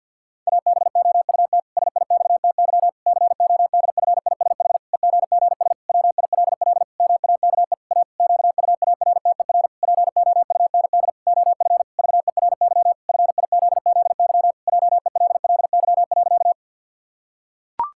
Morse code, heard as a keyed tone, about 51 wpm, 700 Hz. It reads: ABOUT SIXTY CODFISH EGGS WILL MAKE A QUARTER POUND OF VERY FIZZY JELLY.
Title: Pangram Challenge v2 - 50wpm